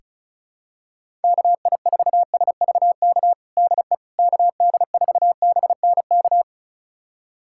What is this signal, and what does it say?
Morse code, heard as a keyed tone, about 36 wpm, 700 Hz. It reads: KI4SVK DE KD4BNK
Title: USA Callsign DE Callsign - 6 Characters - 35wpm